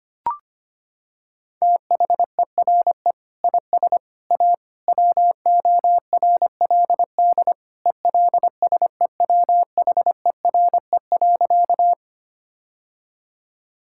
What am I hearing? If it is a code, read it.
Morse code, heard as a keyed tone, about 25 wpm, 700 Hz. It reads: THERE IS A WORLD ELSEWHERE.
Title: Sentences from Top 1600 Words 25wpm